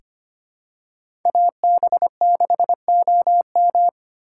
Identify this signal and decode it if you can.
Morse code, heard as a keyed tone, about 25 wpm, 700 Hz. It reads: AB6OM
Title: USA Call Signs - 6x Character Spacing - 25wpm